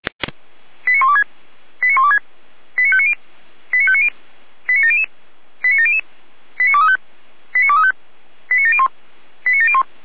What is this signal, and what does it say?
Five-tone selective call (ZVEI), heard as quick runs of five tones, tones of about 70 ms. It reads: 89126 89126 89590 89590 89700 89700 89235 89235 89701 89701